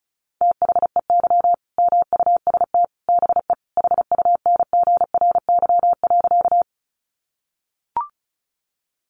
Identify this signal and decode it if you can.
Morse code, heard as a keyed tone, about 34 wpm, 700 Hz. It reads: THEY MUST BE HUNGRY.
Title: Sentences from Top 1400 Words 35wpm